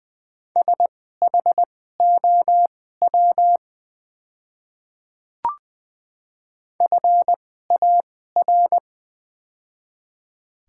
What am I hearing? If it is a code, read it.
Morse code, heard as a keyed tone, about 20 wpm, spaced as for 14 wpm, 700 Hz. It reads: SHOW FAR
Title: 2x Character Spacing - Top 300 Words - 20wpm